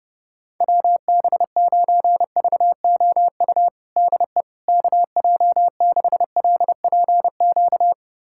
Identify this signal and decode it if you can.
Morse code, heard as a keyed tone, about 30 wpm, 700 Hz. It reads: WB9VOU DE KJ6LPQ
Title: USA Callsign DE Callsign - 6 Characters - 30wpm